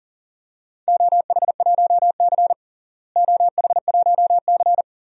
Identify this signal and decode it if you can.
Morse code, heard as a keyed tone, about 40 wpm, 700 Hz. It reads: OH1C OH1C
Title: DXCC-100 Call Signs - 4 Characters - Repeated 3x - 40wpm